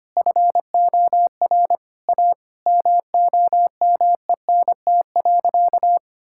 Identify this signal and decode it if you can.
Morse code, heard as a keyed tone, about 25 wpm, 700 Hz. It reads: FOR A MOMENT.
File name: practice / 25wpm / Sentences with 3-Word Phrases - 25wpm